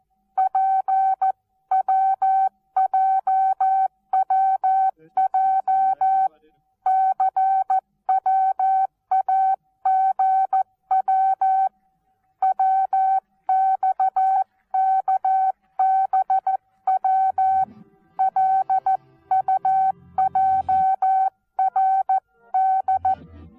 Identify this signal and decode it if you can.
Morse code, heard as a keyed tone, about 14 wpm, 760 Hz. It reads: PWJWJCWAGW WXKBWLUJRD
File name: Onlinelehrgang / morsen / DJ4UF Morsen19